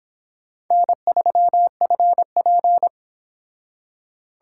Morse code, heard as a keyed tone, about 25 wpm, 700 Hz. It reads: N3FP